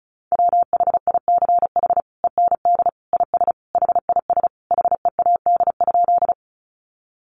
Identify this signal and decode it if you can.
Morse code, heard as a keyed tone, about 34 wpm, 700 Hz. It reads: WHICH END IS HIS HEAD?
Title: Sentences from Top 300 Words 35wpm